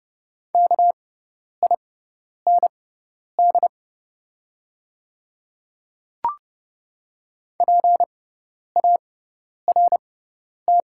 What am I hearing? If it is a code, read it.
Morse code, heard as a keyed tone, about 30 wpm, spaced as for 10 wpm, 700 Hz. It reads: KIND PART
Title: 6x Character Spacing - Top 200 Words - 30wpm